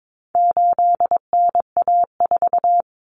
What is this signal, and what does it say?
Morse code, heard as a keyed tone, about 22 wpm, 700 Hz. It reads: 8NA4
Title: Common Cut Numbers - 4-Digit 22wpm